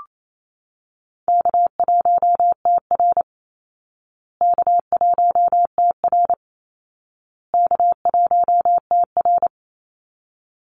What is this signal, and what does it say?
Morse code, heard as a keyed tone, about 28 wpm, 700 Hz. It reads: K1TR K1TR K1TR
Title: Easy Call Signs Repeated 3x - 28wpm